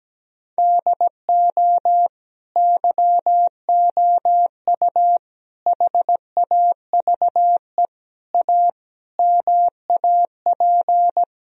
Morse code, heard as a keyed tone, about 17 wpm, 700 Hz. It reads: DO YOU HAVE A MAP
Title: Sentences from Top 1000 Words - Review Part 2 - 17wpm